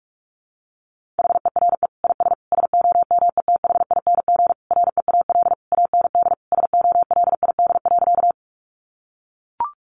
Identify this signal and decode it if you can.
Morse code, heard as a keyed tone, about 44 wpm, 700 Hz. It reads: HERE IS SOMETHING REAL AND SOLID.
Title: Sentences from Top 1500 Words 45wpm